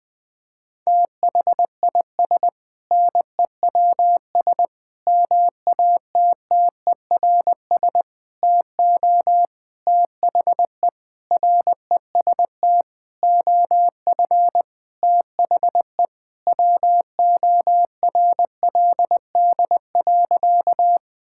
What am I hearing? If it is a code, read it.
Morse code, heard as a keyed tone, about 20 wpm, 700 Hz. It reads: THIS NEWS MATTERS TO THE REST OF THE WORLD.
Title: Sentences with 6-Word Phrases - 20wpm